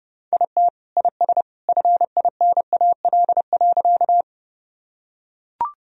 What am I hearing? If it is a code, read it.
Morse code, heard as a keyed tone, about 30 wpm, 700 Hz. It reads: IT IS FINAL.
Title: Sentences from Top 1000 Words - Review Part 2 - 30wpm